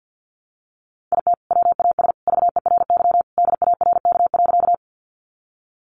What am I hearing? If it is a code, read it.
Morse code, heard as a keyed tone, about 50 wpm, 700 Hz. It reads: IT WAS VERY DARK.
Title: Sentences from Top 500 Words 50wpm